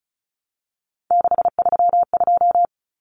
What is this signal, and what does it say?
Morse code, heard as a keyed tone, about 34 wpm, 700 Hz. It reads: 632
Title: Numbers - 3-Digit 35wpm